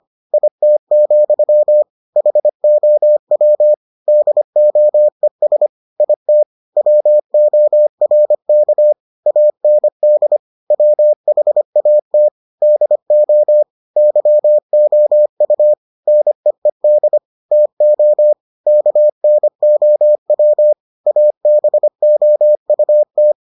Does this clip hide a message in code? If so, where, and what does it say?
Yes: Morse code from 0.3 to 23.3 s, heard as a keyed tone, about 25 wpm, 600 Hz. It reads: IT, HOW DOES IT WORK AND WHAT DO YOU NEED TO KNOW ABOUT